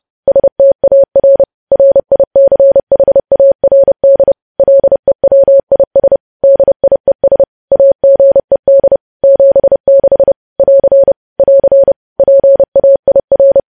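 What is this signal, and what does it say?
Morse code, heard as a keyed tone, about 30 wpm, 550 Hz. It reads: STAR RICHARD LEWIS DIES AGED 76 <AR> <AR> PAIR
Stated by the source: News headlines sent in Morse at 30 wpm, preceded by VVV.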